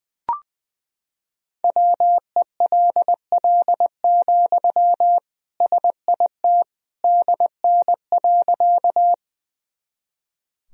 Morse code, heard as a keyed tone, about 21 wpm, 700 Hz. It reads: WELL, SIT DN.
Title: Sentences from Top 500 Words and Abbreviations 20wpm